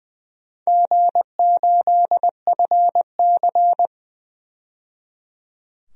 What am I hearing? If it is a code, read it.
Morse code, heard as a keyed tone, about 20 wpm, 700 Hz. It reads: G8FC
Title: DXCC-100 Call Signs - 4 Characters - 20wpm